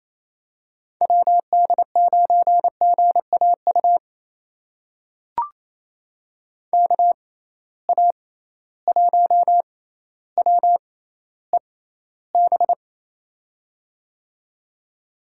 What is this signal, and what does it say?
Morse code, heard as a keyed tone, about 28 wpm, spaced as for 14 wpm, 700 Hz. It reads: WD9GAU KA1WEB
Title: USA Call Signs - 6 Characters - 6x Character Spacing - 28wpm